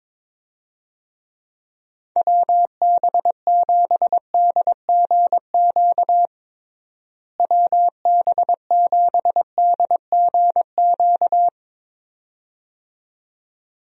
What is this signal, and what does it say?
Morse code, heard as a keyed tone, about 22 wpm, 700 Hz. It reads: WB7DGQ WB7DGQ
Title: USA Call Signs - 6 Characters - Repeated 2x - 22wpm